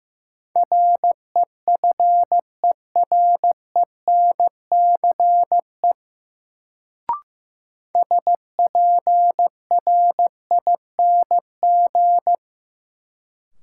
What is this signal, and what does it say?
Morse code, heard as a keyed tone, about 16 wpm, 700 Hz. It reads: REFERENCE SPRING
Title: Sentences from Top 1200 Words - New Word Drill - 15wpm